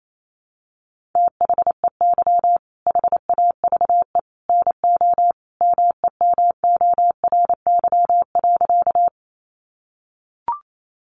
Morse code, heard as a keyed tone, about 28 wpm, 700 Hz. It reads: THEY HAVE NO MEMORY.
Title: Sentences from Top 1300 Words 28wpm